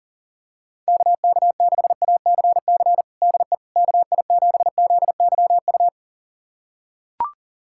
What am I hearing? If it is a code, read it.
Morse code, heard as a keyed tone, about 40 wpm, 700 Hz. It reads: KK6ACC DE KI7ZYU
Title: USA Callsign DE Callsign - 6 Characters - 40wpm